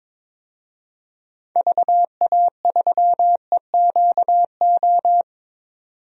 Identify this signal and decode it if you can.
Morse code, heard as a keyed tone, about 22 wpm, 700 Hz. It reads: VA3EQO
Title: DXCC-100 Call Signs - 6 Characters - 22wpm